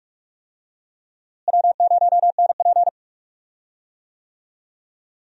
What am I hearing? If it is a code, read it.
Morse code, heard as a keyed tone, about 42 wpm, 700 Hz. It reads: W0NP
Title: USA Call Signs - 4 Characters - Repeated 2x - 45wpm